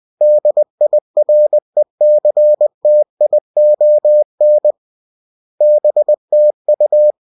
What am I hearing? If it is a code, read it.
Morse code, heard as a keyed tone, about 20 wpm, 600 Hz. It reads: DIRECTION BTU